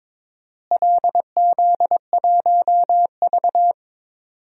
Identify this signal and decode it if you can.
Morse code, heard as a keyed tone, about 22 wpm, 700 Hz. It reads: LZ1V
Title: DXCC-100 Call Signs - 4 Characters - Repeated 2x - 22wpm